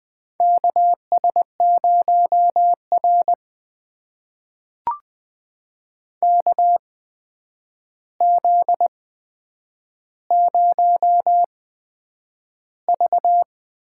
Morse code, heard as a keyed tone, about 20 wpm, spaced as for 9 wpm, 700 Hz. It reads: KS0R KZ0V
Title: USA Call Signs - 4 Characters - 8x Character Spacing - 20wpm